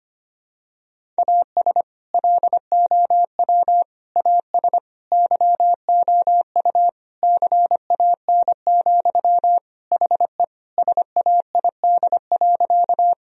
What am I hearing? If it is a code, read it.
Morse code, heard as a keyed tone, about 25 wpm, 700 Hz. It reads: AS LOW AS YOU CAN, HE SAID.
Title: Sentences from Top 1000 Words - Review Part 1 - 25wpm